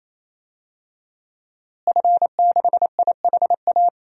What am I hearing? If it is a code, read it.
Morse code, heard as a keyed tone, about 28 wpm, 700 Hz. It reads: F6IHA
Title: DXCC-100 Call Signs - 5 Characters - Repeated 3x - 28wpm